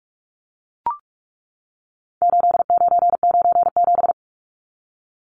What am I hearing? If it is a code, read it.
Morse code, heard as a keyed tone, about 44 wpm, 700 Hz. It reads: 8997
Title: Numbers - 4-Digit 45wpm